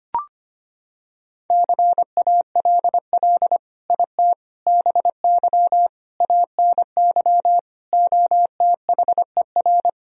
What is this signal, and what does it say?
Morse code, heard as a keyed tone, about 25 wpm, 700 Hz. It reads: CALL IT BY ANY OTHER
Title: Sentences from Top 1000 Words - Review Part 4 - 25wpm